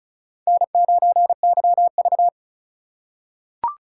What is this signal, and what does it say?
Morse code, heard as a keyed tone, about 34 wpm, 700 Hz. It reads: N9YV
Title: USA Call Signs - 2x Character Spacing - 35wpm